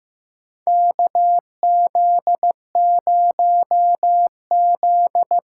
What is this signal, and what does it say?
Morse code, heard as a keyed tone, about 15 wpm, 700 Hz. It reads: KZ0Z
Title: USA Call Signs - 6x Character Spacing - 15wpm